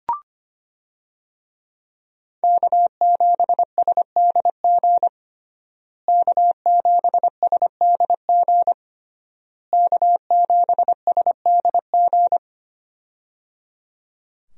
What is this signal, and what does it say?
Morse code, heard as a keyed tone, about 25 wpm, 700 Hz. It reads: K7SDG K7SDG K7SDG
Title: USA Call Signs - 5 Characters - Repeated 3x - 25wpm